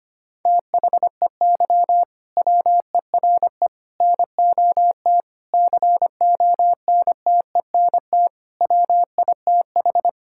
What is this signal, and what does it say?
Morse code, heard as a keyed tone, about 25 wpm, 700 Hz. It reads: THEY WERE NOT CONTENT WITH
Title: Sentences from Top 1000 Words - Review Part 6 - 25wpm